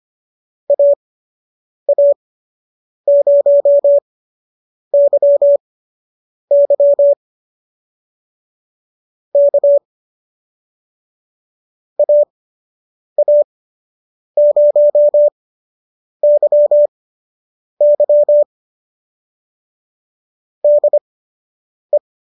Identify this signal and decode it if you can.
Morse code, heard as a keyed tone, about 25 wpm, spaced as for 8 wpm, 590 Hz. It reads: AA0YY K AA0YY DE